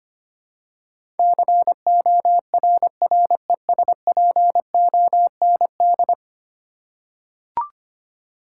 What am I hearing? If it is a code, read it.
Morse code, heard as a keyed tone, about 25 wpm, 700 Hz. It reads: CORRESPOND
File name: 10 Letter Words - First 3 Letters Given - 25wpm